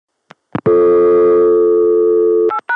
Dial tone (North America), 350 + 440 Hz.